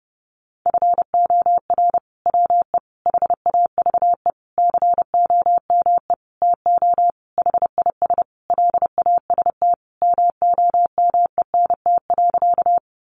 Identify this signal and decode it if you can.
Morse code, heard as a keyed tone, about 30 wpm, 700 Hz. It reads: FOR WE HAVE COME TO HIS LAST MOMENT.
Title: Sentences from Top 1000 Words 30wpm